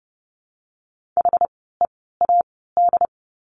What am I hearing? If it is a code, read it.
Morse code, heard as a keyed tone, about 30 wpm, spaced as for 17 wpm, 700 Hz. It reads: HEAD
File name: Lesson 17f - Letter u - Words 30wpm